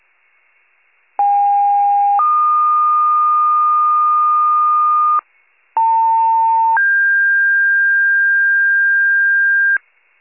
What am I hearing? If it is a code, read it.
Two-tone page, heard as a one-second tone, then a three-second tone, about 810 then 1220 Hz, about 900 then 1600 Hz. It is heard from 1.2 to 9.8 s.